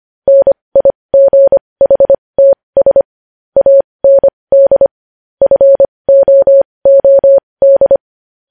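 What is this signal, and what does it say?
Morse code, heard as a keyed tone, about 25 wpm, spaced as for 20 wpm, 550 Hz. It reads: NIGHTS AND FOOD
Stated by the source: Morse code news headlines sent at 20 wpm.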